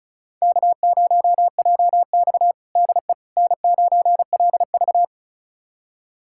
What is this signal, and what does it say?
Morse code, heard as a keyed tone, about 36 wpm, 700 Hz. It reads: K0JX DE N9LV
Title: USA Callsign DE Callsign - 4 Characters - 35wpm